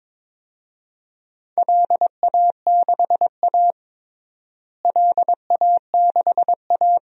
Morse code, heard as a keyed tone, about 22 wpm, 700 Hz. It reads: LA6A LA6A
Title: DXCC-100 Call Signs - 4 Characters - Repeated 3x - 22wpm